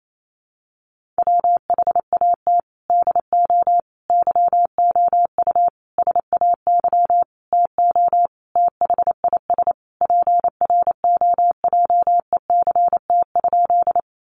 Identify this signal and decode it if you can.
Morse code, heard as a keyed tone, about 28 wpm, 700 Hz. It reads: WHAT DO YOU SAY TO THIS PROJECT?
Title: Sentences from Top 1400 Words 28wpm